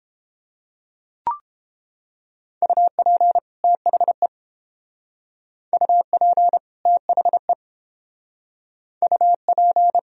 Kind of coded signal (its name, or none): Morse code